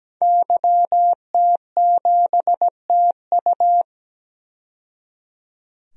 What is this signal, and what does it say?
Morse code, heard as a keyed tone, about 17 wpm, 700 Hz. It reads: YT7TU
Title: DXCC-100 Call Signs - 5 Characters - 17wpm